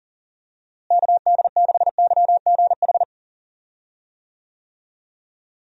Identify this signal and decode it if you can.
Morse code, heard as a keyed tone, about 40 wpm, 700 Hz. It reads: KD6YGH
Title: USA Call Signs - 6 Characters - Repeated 2x - 40wpm